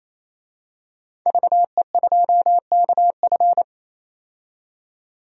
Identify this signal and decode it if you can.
Morse code, heard as a keyed tone, about 28 wpm, 700 Hz. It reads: VE2KF
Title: DXCC-100 Call Signs - 5 Characters - 28wpm